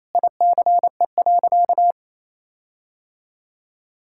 Morse code, heard as a keyed tone, about 28 wpm, 700 Hz. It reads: ICE.
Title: Sentences from Top 1600 Words 28wpm